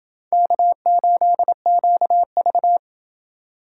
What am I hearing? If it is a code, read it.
Morse code, heard as a keyed tone, about 29 wpm, 700 Hz. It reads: K8QV